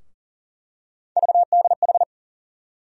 Morse code, heard as a keyed tone, about 40 wpm, 700 Hz. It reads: VDH
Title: USA Call Sign Postfixes - 3 Letters 40wpm